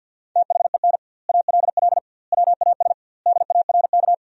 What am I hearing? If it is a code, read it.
Morse code, heard as a keyed tone, about 50 wpm, 700 Hz. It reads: THEN ALL WAS DARK
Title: 2x Word-Spacing - Sentences from Top 500 Words - 50wpm